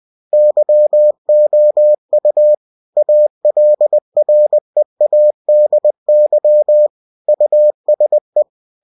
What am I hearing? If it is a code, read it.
Morse code, heard as a keyed tone, about 20 wpm, 600 Hz. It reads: YOU ALREADY USE
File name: cw-books / winnie-the-pooh / winnie-20wpm-page-0117